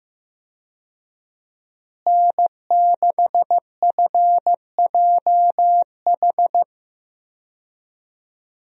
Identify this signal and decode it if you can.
Morse code, heard as a keyed tone, about 15 wpm, 700 Hz. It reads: N6FJH
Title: USA Call Signs - 5 Characters - Repeated 3x - 15wpm